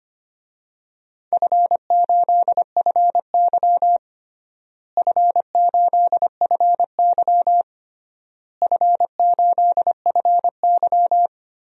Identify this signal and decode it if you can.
Morse code, heard as a keyed tone, about 25 wpm, 700 Hz. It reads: F8FY F8FY F8FY
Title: DXCC-100 Call Signs - 4 Characters - Repeated 3x - 25wpm